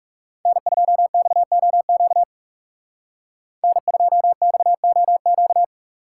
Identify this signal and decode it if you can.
Morse code, heard as a keyed tone, about 35 wpm, 700 Hz. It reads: N2XOQ N2XOQ
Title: Speed Racing - USA Call Signs 30wpm